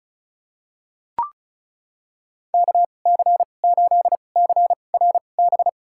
Morse code, heard as a keyed tone, about 36 wpm, spaced as for 25 wpm, 700 Hz. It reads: KC8CRB
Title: USA Call Signs - 6 Characters - 2x Character Spacing - 35wpm